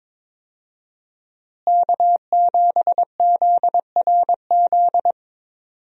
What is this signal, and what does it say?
Morse code, heard as a keyed tone, about 22 wpm, 700 Hz. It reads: K7ZRZ
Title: USA Call Signs - 5 Characters - Repeated 2x - 22wpm